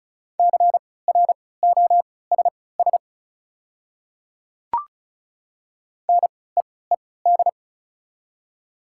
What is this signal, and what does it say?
Morse code, heard as a keyed tone, about 36 wpm, spaced as for 20 wpm, 700 Hz. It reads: CROSS NEED
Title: 3x Character Spacing - Top 300 Words - 35wpm